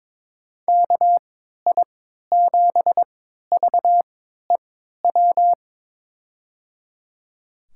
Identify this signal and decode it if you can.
Morse code, heard as a keyed tone, about 22 wpm, spaced as for 13 wpm, 700 Hz. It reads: KI7VEW
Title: USA Call Signs - 3x Character Spacing - 22wpm